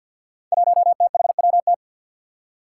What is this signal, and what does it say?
Morse code, heard as a keyed tone, about 52 wpm, 700 Hz. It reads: 1THWT
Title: Five Challenging Letter-Number Combinations 50wpm